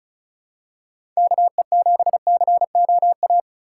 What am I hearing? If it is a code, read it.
Morse code, heard as a keyed tone, about 36 wpm, 700 Hz. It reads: KE7COA
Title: USA Call Signs - 6 Characters - 2x Character Spacing - 35wpm